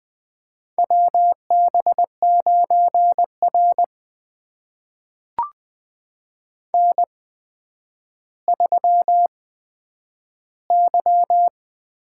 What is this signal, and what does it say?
Morse code, heard as a keyed tone, about 20 wpm, spaced as for 10 wpm, 700 Hz. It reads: WB9R N3Y
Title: USA Call Signs - 8x Character Spacing - 20wpm